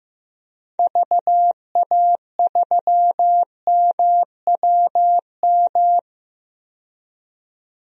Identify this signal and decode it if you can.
Morse code, heard as a keyed tone, about 15 wpm, 700 Hz. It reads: VA3MWM
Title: DXCC-100 Call Signs - 6 Characters - 15wpm